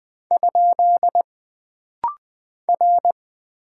Morse code, heard as a keyed tone, about 20 wpm, 700 Hz. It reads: ? R
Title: Single Character (Letter:Number:Punctuation:Prosign) - Rapid-Fire - 20wpm